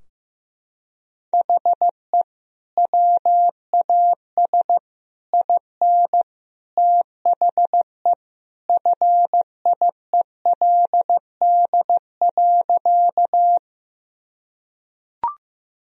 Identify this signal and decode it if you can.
Morse code, heard as a keyed tone, about 15 wpm, 700 Hz. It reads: HE WAS IN THE FIELD.
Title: Sentences from Top 1000 Words - Review Part 2 - 15wpm